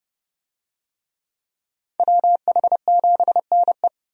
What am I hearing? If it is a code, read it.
Morse code, heard as a keyed tone, about 30 wpm, 700 Hz. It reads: WH7NE
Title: USA Call Signs - 5 Characters - 30wpm